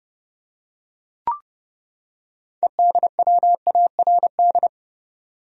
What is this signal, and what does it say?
Morse code, heard as a keyed tone, about 31 wpm, 700 Hz. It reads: EDWARD
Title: USA Common Names - 30wpm